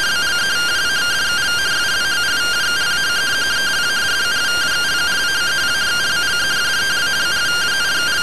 Two-tone teleprinter signal, about 170 Hz shift.
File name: findings / CT SAGITAL 27.gen-21 rtty rx